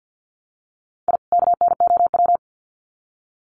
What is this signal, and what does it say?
Morse code, heard as a keyed tone, about 50 wpm, 700 Hz. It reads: I KNOW
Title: Sets of 2 Words 50wpm